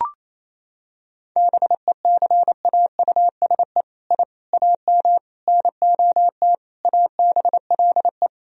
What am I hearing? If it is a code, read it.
Morse code, heard as a keyed tone, about 28 wpm, 700 Hz. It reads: BECAUSE I AM NOT ABLE
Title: Sentences from Top 1000 Words - Review Part 3 - 28wpm